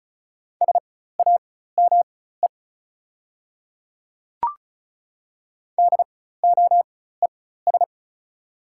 Morse code, heard as a keyed tone, about 36 wpm, spaced as for 16 wpm, 690 Hz. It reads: SAME DOES
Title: 4x Character Spacing - Top 200 Words - 35wpm